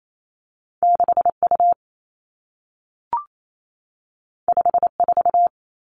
Morse code, heard as a keyed tone, about 28 wpm, 700 Hz. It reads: 6 U 5 4
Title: Two Challenging Letter-Number Combinations 28wpm